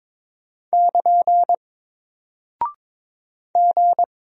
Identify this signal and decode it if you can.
Morse code, heard as a keyed tone, about 22 wpm, 700 Hz. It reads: <KN> G
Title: Single Character (Letter:Number:Punctuation:Prosign) Alt 22wpm